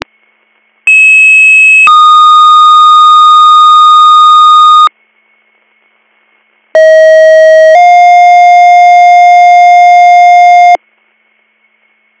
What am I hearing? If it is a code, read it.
Two-tone page, heard as a one-second tone, then a three-second tone, about 2600 then 1230 Hz, about 640 then 710 Hz. It is heard from 0.9 to 10.8 s.